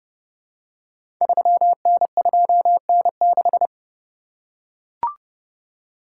Morse code, heard as a keyed tone, about 30 wpm, 700 Hz. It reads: 3N2N6
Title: Code Groups - 5 Letter:Numbers - 30wpm